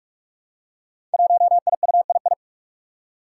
Morse code, heard as a keyed tone, about 44 wpm, 700 Hz. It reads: 1IUII
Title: Five Challenging Letter-Number Combinations 45wpm